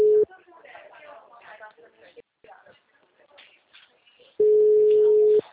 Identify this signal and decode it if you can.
Ringback tone (Europe, 425 Hz), with speech.
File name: out- -2018- - - .37540